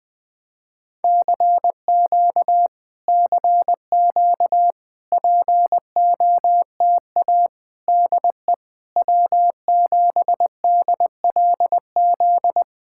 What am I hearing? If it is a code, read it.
Morse code, heard as a keyed tone, about 20 wpm, 700 Hz. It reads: CQ CQ POTA DE W7DLZ
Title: POTA QSO Practice - 20wpm